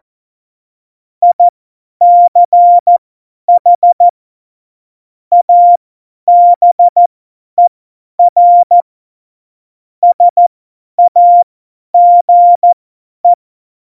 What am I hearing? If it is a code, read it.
Morse code, heard as a keyed tone, about 14 wpm, spaced as for 10 wpm, 700 Hz. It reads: ICH ABER SAGE
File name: Morsekurs / Witze / Witz 01. Dezember 2020, 50 F70